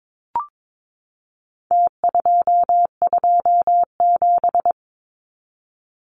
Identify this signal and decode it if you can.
Morse code, heard as a keyed tone, about 22 wpm, 700 Hz. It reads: T227
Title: Numbers - Contest Style - 4-Digit 22wpm